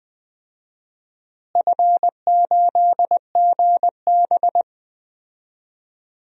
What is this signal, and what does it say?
Morse code, heard as a keyed tone, about 20 wpm, 700 Hz. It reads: F8GB
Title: DXCC-100 Call Signs - 4 Characters - Repeated 3x - 20wpm